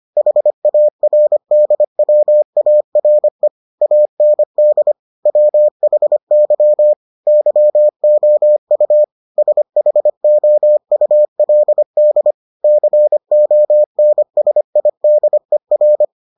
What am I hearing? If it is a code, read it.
Morse code, heard as a keyed tone, about 25 wpm, 600 Hz. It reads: HARDWARE AND WHY YOU SHOULD CONSIDER